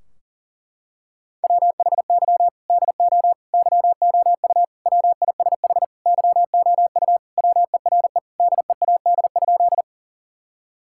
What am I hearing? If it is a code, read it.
Morse code, heard as a keyed tone, about 40 wpm, 700 Hz. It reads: WHY DO YOU WISH YOU WERE DEAD?
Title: Sentences from Top 1600 Words 40wpm